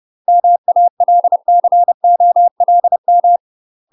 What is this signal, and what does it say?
Morse code, heard as a keyed tone, about 30 wpm, 700 Hz. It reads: MALCOLM
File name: UK MASTER QSOs TRAINING / 30WPM / UK MASTER QSOs 0033